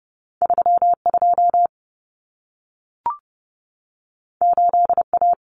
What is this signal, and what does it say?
Morse code, heard as a keyed tone, about 30 wpm, 700 Hz. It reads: 32 8A
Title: Common Cut Numbers - 2-Digit 30wpm